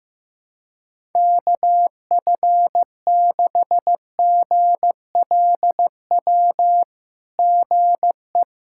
Morse code, heard as a keyed tone, about 16 wpm, 690 Hz. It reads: KF6GLW GE
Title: SOTA QSO Practice 15wpm